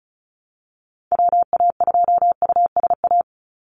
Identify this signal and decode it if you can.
Morse code, heard as a keyed tone, about 34 wpm, 700 Hz. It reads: WA2USA
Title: NA SOTA Activator Callsigns 35wpm